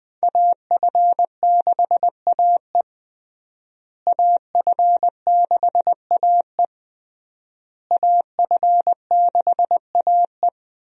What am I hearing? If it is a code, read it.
Morse code, heard as a keyed tone, about 20 wpm, 700 Hz. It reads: AF6AE AF6AE AF6AE
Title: USA Call Signs - 5 Characters - Repeated 3x - 20wpm